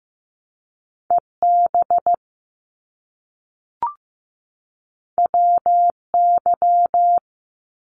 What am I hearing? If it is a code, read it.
Morse code, heard as a keyed tone, about 15 wpm, 700 Hz. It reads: EB WY
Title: ARRL Sections 15wpm